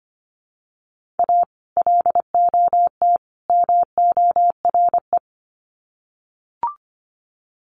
Morse code, heard as a keyed tone, about 25 wpm, 700 Hz. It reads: A LOT MORE
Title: Sets of 3 Words 25wpm